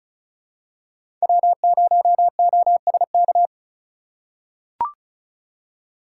Morse code, heard as a keyed tone, about 36 wpm, 700 Hz. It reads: W0OSK
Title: USA Call Signs - 3x Character Spacing - 35wpm